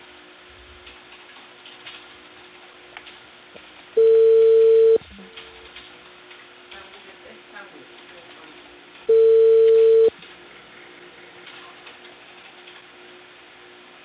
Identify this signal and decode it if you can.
Ringback tone (China, 450 Hz), with speech.